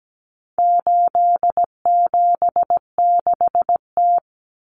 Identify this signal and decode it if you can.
Morse code, heard as a keyed tone, about 17 wpm, 700 Hz. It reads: 876T
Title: Common Cut Numbers - 4-Digit 17wpm